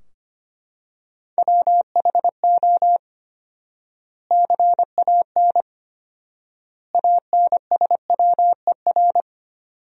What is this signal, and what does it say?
Morse code, heard as a keyed tone, about 25 wpm, 700 Hz. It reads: WHO CAN ANSWER
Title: 4x Word-Spacing - Sentences from Top 300 Words - 25wpm